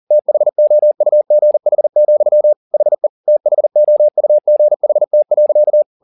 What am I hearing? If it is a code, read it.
Morse code, heard as a keyed tone, about 40 wpm, 600 Hz. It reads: THOUGH, HE THOUGHT.
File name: the old man and the sea 2 40wpm